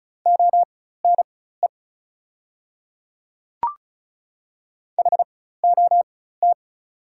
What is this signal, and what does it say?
Morse code, heard as a keyed tone, about 36 wpm, spaced as for 16 wpm, 690 Hz. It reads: ONE HOT
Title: 4x Character Spacing - Top 300 Words - 35wpm